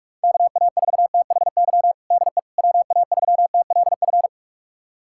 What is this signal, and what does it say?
Morse code, heard as a keyed tone, about 41 wpm, 690 Hz. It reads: KA4THY DE WA3TLF
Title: USA Callsign DE Callsign 45wpm